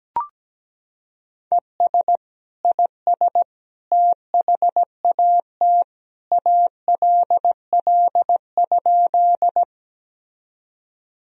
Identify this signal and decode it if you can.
Morse code, heard as a keyed tone, about 17 wpm, 700 Hz. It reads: ES IS THAT ALL?
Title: Sentences from Top 100 Words and Abbreviations 17wpm